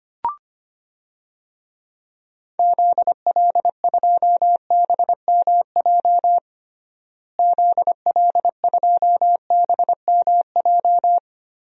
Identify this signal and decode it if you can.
Morse code, heard as a keyed tone, about 25 wpm, 700 Hz. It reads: ZL2BMJ ZL2BMJ
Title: DXCC-100 Call Signs - 6 Characters - Repeated 3x - 25wpm